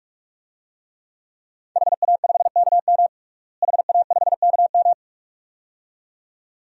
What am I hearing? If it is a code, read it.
Morse code, heard as a keyed tone, about 43 wpm, 700 Hz. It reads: HA5KM HA5KM
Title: DXCC-100 Call Signs - 5 Characters - Repeated 2x - 45wpm